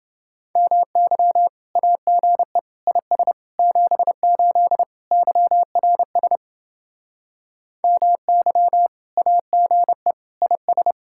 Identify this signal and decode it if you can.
Morse code, heard as a keyed tone, about 28 wpm, 700 Hz. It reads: MY AGE IS 78 YRS MY AGE IS